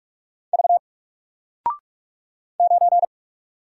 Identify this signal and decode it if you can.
Morse code, heard as a keyed tone, about 44 wpm, 700 Hz. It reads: V 9
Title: Single Letter-Number 45wpm